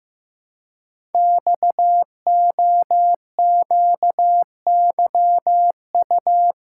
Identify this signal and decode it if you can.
Morse code, heard as a keyed tone, about 15 wpm, 700 Hz. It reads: XOQYU
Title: Code Groups - 5 Letters - 15wpm